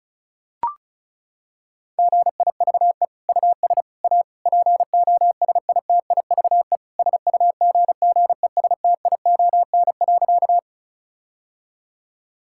Morse code, heard as a keyed tone, about 34 wpm, 700 Hz. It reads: GIVE US A POSITIVE SUGGESTION.
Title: Sentences from Top 1600 Words 35wpm